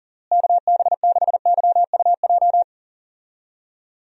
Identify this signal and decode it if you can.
Morse code, heard as a keyed tone, about 40 wpm, 700 Hz. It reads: KB6YUJ
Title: USA Call Signs - 6 Characters - Repeated 2x - 40wpm